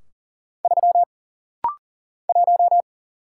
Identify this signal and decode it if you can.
Morse code, heard as a keyed tone, about 40 wpm, 700 Hz. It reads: 3 1
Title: Single Letter-Number - Rapid-Fire 40wpm